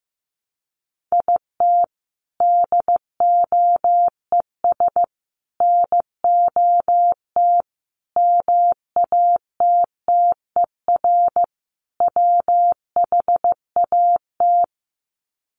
Morse code, heard as a keyed tone, about 15 wpm, 700 Hz. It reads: IT DOES NOT MATTER WHAT
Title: Sets of 5 Words Encore 15wpm